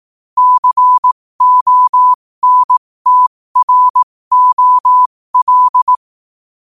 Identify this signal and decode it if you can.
Morse code, heard as a keyed tone, about 18 wpm, spaced as for 15 wpm, 1000 Hz. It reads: CONTROL
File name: qotdfiles / 15wpm